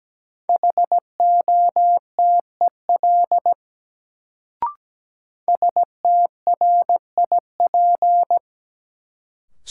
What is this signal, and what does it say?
Morse code, heard as a keyed tone, about 17 wpm, 700 Hz. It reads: HOTEL STRIP
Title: Sentences from Top 1600 Words - New Word Drill - 17wpm